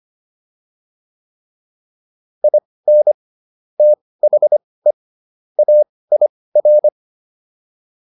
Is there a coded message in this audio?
Yes: Morse code from 2.4 to 6.9 s, heard as a keyed tone, about 25 wpm, spaced as for 18 wpm, 600 Hz. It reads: IN THE AIR